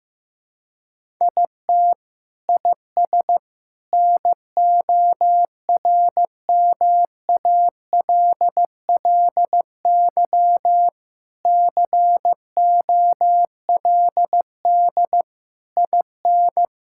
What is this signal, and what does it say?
Morse code, heard as a keyed tone, about 15 wpm, 700 Hz. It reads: IT IS NORMALLY COLD IN
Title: Sentences from Top 1000 Words - Review Part 6 - 15wpm